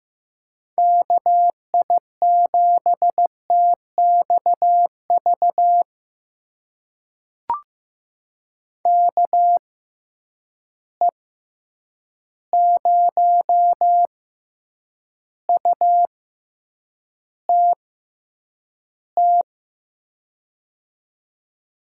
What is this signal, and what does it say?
Morse code, heard as a keyed tone, about 15 wpm, spaced as for 8 wpm, 700 Hz. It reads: KI7TXV KE0UTT
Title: USA Call Signs - 6 Characters - 6x Character Spacing - 15wpm